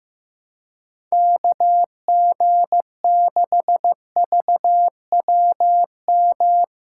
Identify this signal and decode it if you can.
Morse code, heard as a keyed tone, about 15 wpm, 700 Hz. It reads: KG6VWM
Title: USA Call Signs - 3x Character Spacing - 15wpm